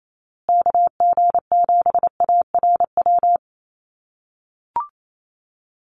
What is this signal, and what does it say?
Morse code, heard as a keyed tone, about 28 wpm, 700 Hz. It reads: KG7ARW
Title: USA Call Signs Repeated 3x 28wpm